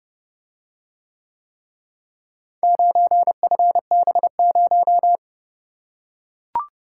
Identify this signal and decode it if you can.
Morse code, heard as a keyed tone, about 29 wpm, 700 Hz. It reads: 9FB0